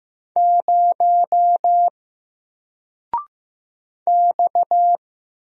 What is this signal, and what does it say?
Morse code, heard as a keyed tone, about 15 wpm, 700 Hz. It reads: 0 X
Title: Single Letter-Number Alt 15wpm